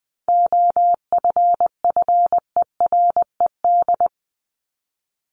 Morse code, heard as a keyed tone, about 20 wpm, 700 Hz. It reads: OFFERED
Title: ED Suffix - Top 100 Words with Base Words 20wpm